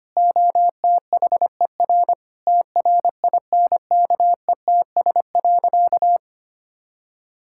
Morse code, heard as a keyed tone, about 25 wpm, 700 Hz. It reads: OTHER TRINKETS.
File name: practice / 25wpm / Sentences with 2-Word Phrases - 25wpm